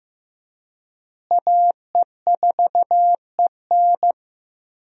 Morse code, heard as a keyed tone, about 15 wpm, 700 Hz. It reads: AE4EN
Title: USA Call Signs - 5 Characters - Repeated 3x - 15wpm